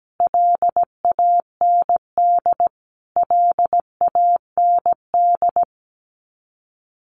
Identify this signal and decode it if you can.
Morse code, heard as a keyed tone, about 17 wpm, 700 Hz. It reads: LAND LAND
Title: Top 200 Words Repeated 17wpm